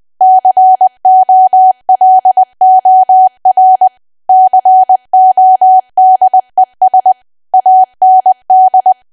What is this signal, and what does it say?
Morse code, heard as a keyed tone, about 20 wpm, 750 Hz. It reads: COLOR CODES AND
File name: cwa / W1AW-Level3 / 20WPM